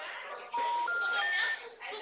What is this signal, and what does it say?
Special information tone, with speech.
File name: out- -2008- - - .13352